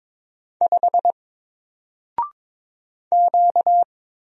Morse code, heard as a keyed tone, about 22 wpm, 700 Hz. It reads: 5 Q
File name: Single-Letter-Number-22wpm